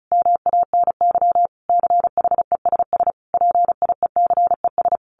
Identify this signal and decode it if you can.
Morse code, heard as a keyed tone, about 34 wpm, 700 Hz. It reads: MANY CHESS PIECES
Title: A Sound Of Thunder 35wpm